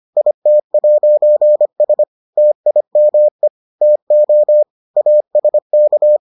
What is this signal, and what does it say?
Morse code, heard as a keyed tone, about 25 wpm, 600 Hz. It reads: IT'S TIME TO ASK